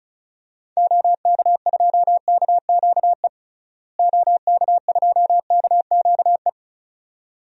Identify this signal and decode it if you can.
Morse code, heard as a keyed tone, about 36 wpm, 700 Hz. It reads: OK2KQE OK2KQE
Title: DXCC-100 Call Signs - 6 Characters - Repeated 2x - 35wpm